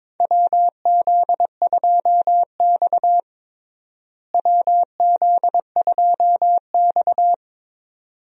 Morse code, heard as a keyed tone, about 22 wpm, 700 Hz. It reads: WZ2X WZ2X
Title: USA Call Signs - 4 Characters - Repeated 2x - 22wpm